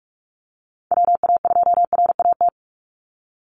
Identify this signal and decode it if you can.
Morse code, heard as a keyed tone, about 44 wpm, 700 Hz. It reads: WA2RAT
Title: USA Call Signs 45wpm